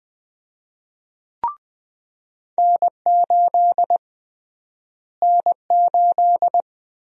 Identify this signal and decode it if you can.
Morse code, heard as a keyed tone, about 20 wpm, 700 Hz. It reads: N8 N8
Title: USA Call Sign Prefixes - Repeated 2x - 20wpm